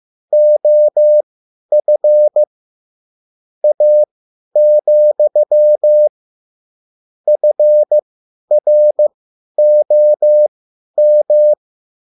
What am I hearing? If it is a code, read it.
Morse code, heard as a keyed tone, about 15 wpm, spaced as for 10 wpm, 600 Hz. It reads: OF A, FROM
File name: QSL / CW lesson